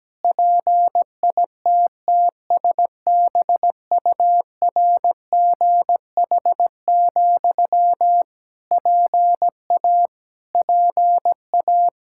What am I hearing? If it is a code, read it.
Morse code, heard as a keyed tone, about 17 wpm, 700 Hz. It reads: PITTSBURGH, PA PA
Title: Practice QSO - Exchange 1 - Beginning - 17wpm